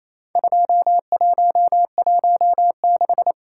Morse code, heard as a keyed tone, about 28 wpm, 700 Hz. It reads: 2116
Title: Worked All Europe DX Contest - QTC Exchange - 28wpm